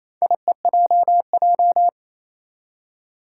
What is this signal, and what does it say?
Morse code, heard as a keyed tone, about 28 wpm, 700 Hz. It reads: IEJJ
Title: Four Challenging Letter-Number Combinations 28wpm